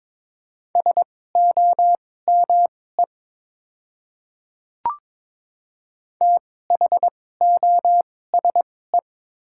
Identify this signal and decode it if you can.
Morse code, heard as a keyed tone, about 22 wpm, spaced as for 16 wpm, 700 Hz. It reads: SOME THOSE
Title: 2x Character Spacing - Top 300 Words - 22wpm